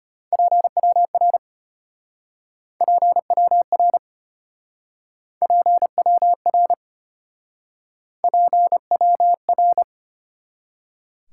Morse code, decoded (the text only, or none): PWR PWR PWR PWR